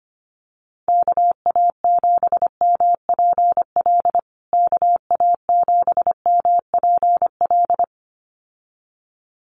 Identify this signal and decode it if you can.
Morse code, heard as a keyed tone, about 25 wpm, 700 Hz. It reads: KA7MPL KA7MPL
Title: USA Call Signs Repeated 2x 25wpm